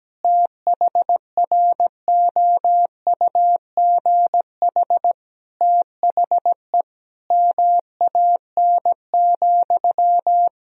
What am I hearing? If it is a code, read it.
Morse code, heard as a keyed tone, about 17 wpm, 700 Hz. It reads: THROUGH THE MAN,
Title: Sentences from Top 1000 Words - Review Part 1 - 17wpm